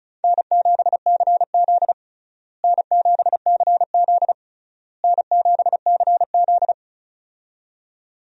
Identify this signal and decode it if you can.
Morse code, heard as a keyed tone, about 34 wpm, 700 Hz. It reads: N7CZ N7CZ N7CZ
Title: USA Call Signs - 4 Characters - Repeated 3x - 35wpm